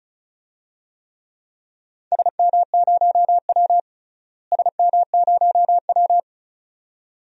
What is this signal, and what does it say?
Morse code, heard as a keyed tone, about 36 wpm, 700 Hz. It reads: SM0W SM0W
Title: DXCC-100 Call Signs - 4 Characters - Repeated 2x - 35wpm